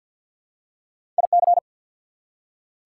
Morse code, heard as a keyed tone, about 48 wpm, 690 Hz. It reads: IC
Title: Prefixes and Suffixes 50wpm